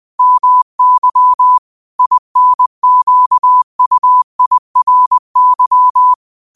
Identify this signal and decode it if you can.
Morse code, heard as a keyed tone, about 20 wpm, 1000 Hz. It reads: MY INQUIRY
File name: qotdfiles / 20wpm